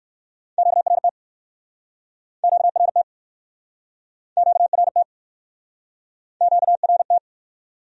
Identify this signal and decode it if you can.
Morse code, heard as a keyed tone, about 61 wpm, 700 Hz. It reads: QRT QRT QRT QRT